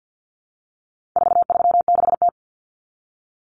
Morse code, heard as a keyed tone, about 50 wpm, 700 Hz. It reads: 436T
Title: Common Cut Numbers - 4-Digit 50wpm